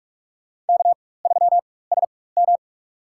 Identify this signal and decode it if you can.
Morse code, heard as a keyed tone, about 46 wpm, spaced as for 21 wpm, 700 Hz. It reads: K3SM
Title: USA Call Signs - 4x Character Spacing - 45wpm